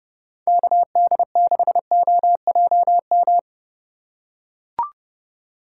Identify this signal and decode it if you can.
Morse code, heard as a keyed tone, about 30 wpm, 700 Hz. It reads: KD6OJM